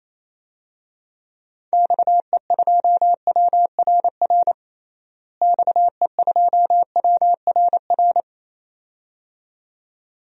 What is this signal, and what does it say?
Morse code, heard as a keyed tone, about 28 wpm, 700 Hz. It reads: XE2WRR XE2WRR
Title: DXCC-100 Call Signs - 6 Characters - Repeated 2x - 28wpm